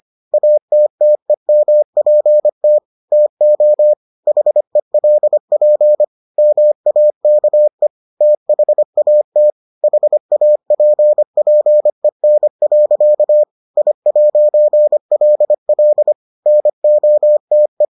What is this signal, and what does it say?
Morse code, heard as a keyed tone, about 25 wpm, 600 Hz. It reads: ATTEMPT TO HELP MAKE THAT HAPPEN. I'LL NOTE